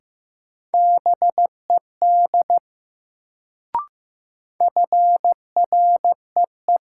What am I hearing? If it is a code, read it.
Morse code, heard as a keyed tone, about 15 wpm, 700 Hz. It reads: BED FREE
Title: Sentences from Top 500 Words - New Word Drill - 15wpm